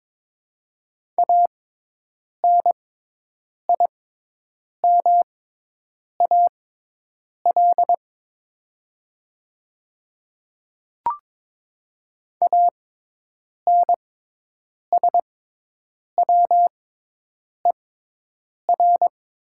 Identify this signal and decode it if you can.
Morse code, heard as a keyed tone, about 22 wpm, spaced as for 8 wpm, 700 Hz. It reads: ANIMAL ANSWER
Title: 6x Character Spacing - Top 300 Words - 22wpm